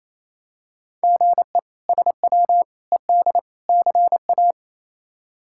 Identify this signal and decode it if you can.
Morse code, heard as a keyed tone, about 27 wpm, 700 Hz. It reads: GE SW ED CA
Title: K1USN SST QSO Practice - 28wpm